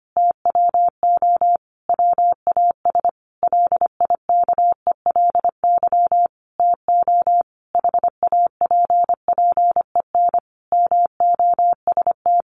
Morse code, heard as a keyed tone, about 25 wpm, 700 Hz. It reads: TWO WAS LIKELY TO HAPPEN MOST
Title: Sentences from Top 1300 Words 25wpm